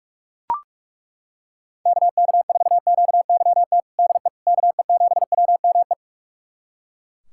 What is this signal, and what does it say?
Morse code, heard as a keyed tone, about 42 wpm, 700 Hz. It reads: KK4QYT DE KE7WME
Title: USA Callsign DE Callsign - 6 Characters - 45wpm